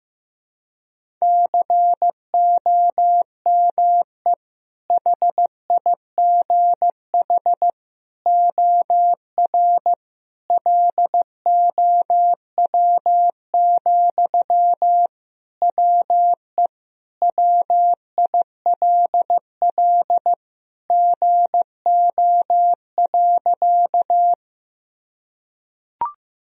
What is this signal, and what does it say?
Morse code, heard as a keyed tone, about 15 wpm, 700 Hz. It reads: COME HIGH OR LOW, WE WILL GO.
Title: Sentences from Top 1000 Words - Review Part 1 - 15wpm